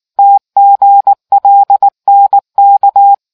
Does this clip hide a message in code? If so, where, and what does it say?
Yes: Morse code from 0.2 to 3.1 s, heard as a keyed tone, about 18 wpm, 800 Hz. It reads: TGLNK